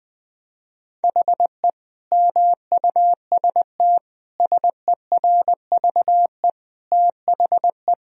Morse code, heard as a keyed tone, about 20 wpm, 700 Hz. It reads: HE MUST SERVE THE
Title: Sentences from Top 1000 Words - Review Part 2 - 20wpm